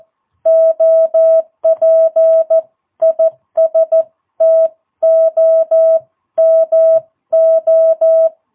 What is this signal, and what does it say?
Morse code, heard as a keyed tone, about 14 wpm, 660 Hz. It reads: OP IS TOMO
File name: jh8cbh / 2022cw